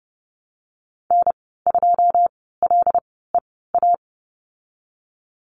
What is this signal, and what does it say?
Morse code, heard as a keyed tone, about 30 wpm, spaced as for 17 wpm, 700 Hz. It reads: N2LEA
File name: Lesson 19g - Letter w - Call Signs 30wpm